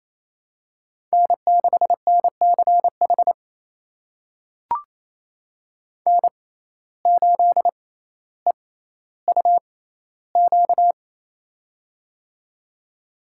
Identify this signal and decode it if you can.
Morse code, heard as a keyed tone, about 28 wpm, spaced as for 14 wpm, 700 Hz. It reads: N6NCH N8EUQ
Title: USA Call Signs - 5 Characters - 6x Character Spacing - 28wpm